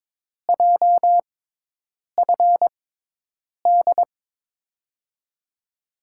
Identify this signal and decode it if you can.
Morse code, heard as a keyed tone, about 22 wpm, spaced as for 8 wpm, 700 Hz. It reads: JFD
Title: USA Call Signs - 5 Characters - 6x Character Spacing - 22wpm